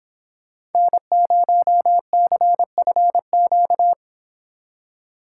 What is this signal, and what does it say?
Morse code, heard as a keyed tone, about 30 wpm, 700 Hz. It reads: N0CFQ